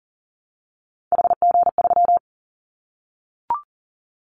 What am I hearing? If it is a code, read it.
Morse code, heard as a keyed tone, about 40 wpm, 700 Hz. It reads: HG3
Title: Callsign Prefixes for Top 50 DXCC - 40wpm